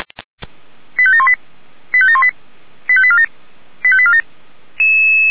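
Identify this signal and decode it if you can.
Five-tone selective call (ZVEI), heard as quick runs of five tones, tones of about 70 ms. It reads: 85718 85718 85847 85847
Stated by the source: ZVEI five-tone call received by SDR, from the audio server of Feuerwehr Orlamünde (f85360).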